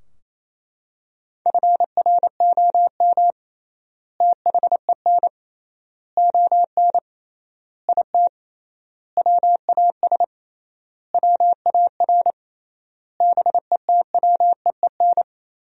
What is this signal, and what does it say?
Morse code, heard as a keyed tone, about 28 wpm, 700 Hz. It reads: FROM THEN ON IT WAS WAR BETWEEN
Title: 3x Word-Spacing - Sentences from Top 400 Words - 28wpm